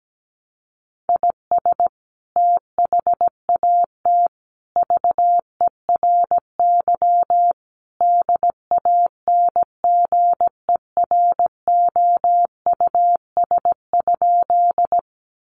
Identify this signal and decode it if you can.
Morse code, heard as a keyed tone, about 17 wpm, 700 Hz. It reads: IS THAT VERY DANGEROUS?
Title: Sentences from Top 1200 Words 17wpm